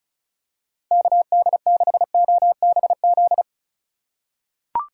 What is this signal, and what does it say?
Morse code, heard as a keyed tone, about 36 wpm, 700 Hz. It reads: KD6OBZ
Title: USA Call Signs - 6 Characters - 3x Character Spacing - 35wpm